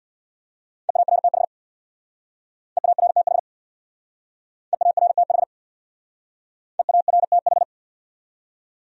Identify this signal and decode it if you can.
Morse code, heard as a keyed tone, about 62 wpm, 700 Hz. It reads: EARTH EARTH EARTH EARTH
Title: Speed-Racing Top 500 Words 50wpm